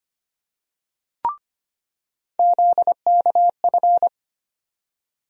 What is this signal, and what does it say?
Morse code, heard as a keyed tone, about 25 wpm, 700 Hz. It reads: ZKF
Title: USA Call Sign Postfixes - 3 Letters 25wpm